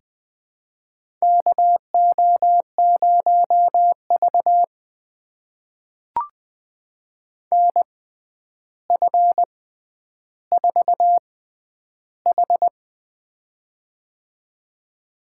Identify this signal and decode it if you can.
Morse code, heard as a keyed tone, about 20 wpm, spaced as for 10 wpm, 700 Hz. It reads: KO0V NF4H
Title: USA Call Signs - 4 Characters - 6x Character Spacing - 20wpm